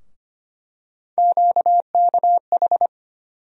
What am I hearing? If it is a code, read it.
Morse code, heard as a keyed tone, about 25 wpm, 700 Hz. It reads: QKH
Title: USA Call Sign Postfixes - 3 Letters 25wpm